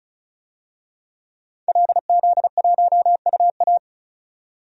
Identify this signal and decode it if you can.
Morse code, heard as a keyed tone, about 34 wpm, 700 Hz. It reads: LZ1UA
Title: DXCC-100 Call Signs - 5 Characters - 35wpm